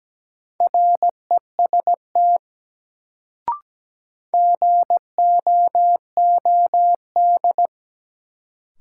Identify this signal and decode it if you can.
Morse code, heard as a keyed tone, about 17 wpm, 700 Hz. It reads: REST GOOD
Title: Sets of 2 Words - New Word Drill - 17wpm